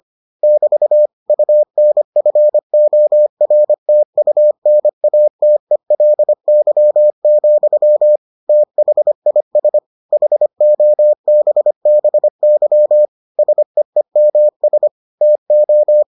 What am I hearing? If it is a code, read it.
Morse code, heard as a keyed tone, about 25 wpm, 600 Hz. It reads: <BT> UNFORTUNATELY, THIS HOBBY SEEMS TO